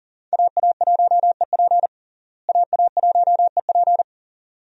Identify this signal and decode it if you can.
Morse code, heard as a keyed tone, about 40 wpm, 700 Hz. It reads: AA1EP AA1EP
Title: USA Call Signs - 5 Characters - Repeated 2x - 40wpm